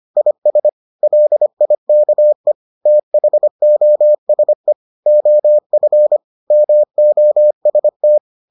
Morse code, heard as a keyed tone, about 25 wpm, 600 Hz. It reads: IS LIKE THOSE OF MOST